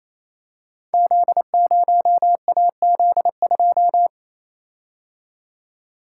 Morse code, heard as a keyed tone, about 27 wpm, 700 Hz. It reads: Z0AZ2
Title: Code Groups - 5 Letter:Numbers - 28wpm